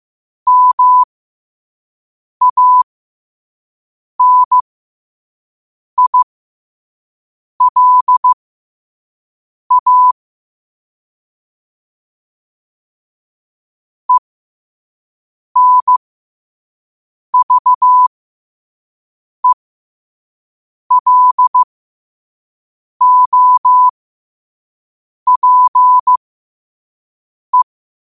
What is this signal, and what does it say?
Morse code, heard as a keyed tone, about 15 wpm, spaced as for 5 wpm, 1000 Hz. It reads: MANILA ENVELOPE
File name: 5wpm